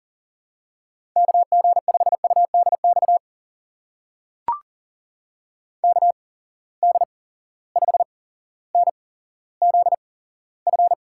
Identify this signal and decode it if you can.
Morse code, heard as a keyed tone, about 41 wpm, spaced as for 17 wpm, 700 Hz. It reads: KG5UDX KD5NZF
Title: USA Call Signs - 6 Characters - 8x Character Spacing - 40wpm